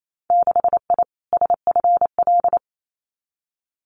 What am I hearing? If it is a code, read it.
Morse code, heard as a keyed tone, about 28 wpm, 690 Hz. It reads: 6I SFL
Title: Winter Field Day Exchanges Encore 28wpm